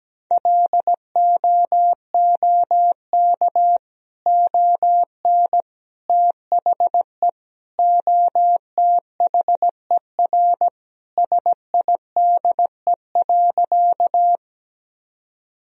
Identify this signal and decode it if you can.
Morse code, heard as a keyed tone, about 17 wpm, 700 Hz. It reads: LOOK ON THE OTHER SIDE.
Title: Sentences from Top 1000 Words - Review Part 1 - 17wpm